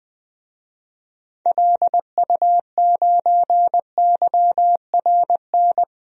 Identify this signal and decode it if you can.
Morse code, heard as a keyed tone, about 20 wpm, 700 Hz. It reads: LU9YRN
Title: DXCC-100 Call Signs - 6 Characters - 20wpm